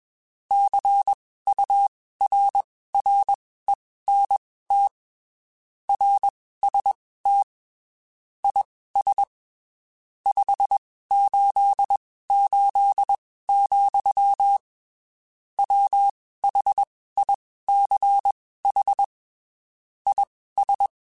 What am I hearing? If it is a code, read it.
Morse code, heard as a keyed tone, about 21 wpm, spaced as for 15 wpm, 800 Hz. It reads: CURRENT RST IS 588, WHICH IS
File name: (spaced) / QSOs Spaced-mp3s / WA7QOX 1 @20 sp